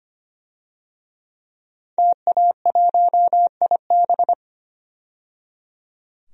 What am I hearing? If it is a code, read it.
Morse code, heard as a keyed tone, about 25 wpm, 700 Hz. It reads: TA1IB
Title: DXCC-100 Call Signs - 5 Characters - 25wpm